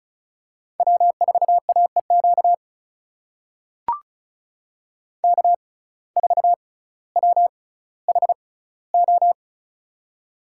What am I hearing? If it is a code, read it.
Morse code, heard as a keyed tone, about 34 wpm, spaced as for 18 wpm, 700 Hz. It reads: W4AEQ K4WHO
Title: USA Call Signs - 5 Characters - 6x Character Spacing - 35wpm